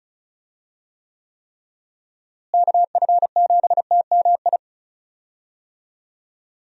Morse code, heard as a keyed tone, about 34 wpm, 700 Hz. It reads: KF7TMI
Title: USA Call Signs - 6 Characters - Repeated 3x - 35wpm